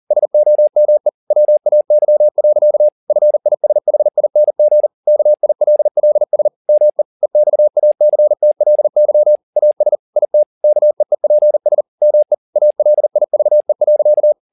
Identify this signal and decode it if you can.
Morse code, heard as a keyed tone, about 40 wpm, 600 Hz. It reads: SOME WAY. FISHING KILLS ME EXACTLY AS IT KEEPS ME ALIVE.
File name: the old man and the sea 2 40wpm